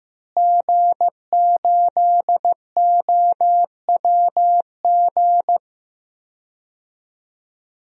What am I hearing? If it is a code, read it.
Morse code, heard as a keyed tone, about 15 wpm, 700 Hz. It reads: G8OWG
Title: DXCC-100 Call Signs - 5 Characters - Repeated 3x - 15wpm